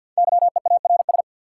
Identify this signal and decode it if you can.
Morse code, heard as a keyed tone, about 53 wpm, 700 Hz. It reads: YEARS
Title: Sentences from Top 200 Words - New Word Drill - 50wpm